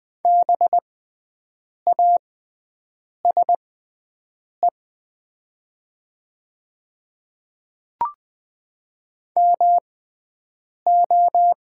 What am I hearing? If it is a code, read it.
Morse code, heard as a keyed tone, about 20 wpm, spaced as for 7 wpm, 700 Hz. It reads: BASE MO
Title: 6x Character Spacing - Top 300 Words - 20wpm